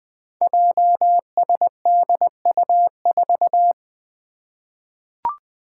Morse code, heard as a keyed tone, about 20 wpm, 700 Hz. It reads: JSDU4
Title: Five Challenging Letter-Number Combinations 20wpm